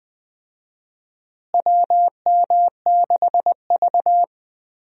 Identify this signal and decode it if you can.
Morse code, heard as a keyed tone, about 20 wpm, 700 Hz. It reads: WM6V
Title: USA Call Signs - 4 Characters - Repeated 2x - 20wpm